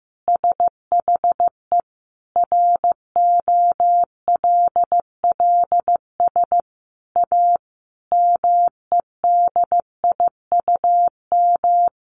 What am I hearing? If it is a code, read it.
Morse code, heard as a keyed tone, about 15 wpm, 700 Hz. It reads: SHE ROLLS A MEDIUM
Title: Easy Phrases 15wpm